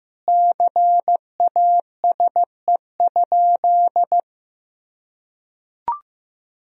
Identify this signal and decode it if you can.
Morse code, heard as a keyed tone, about 16 wpm, 700 Hz. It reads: CASE?
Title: Sentences from Top 1000 Words - Review Part 3 - 15wpm